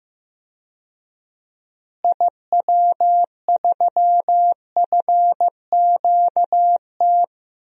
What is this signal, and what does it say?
Morse code, heard as a keyed tone, about 15 wpm, 700 Hz. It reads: IW3FQT
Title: DXCC-100 Call Signs - 6 Characters - Repeated 3x - 15wpm